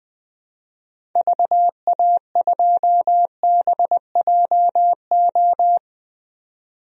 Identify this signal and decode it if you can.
Morse code, heard as a keyed tone, about 20 wpm, 700 Hz. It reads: VA2BJO
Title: DXCC-100 Call Signs - 6 Characters - 20wpm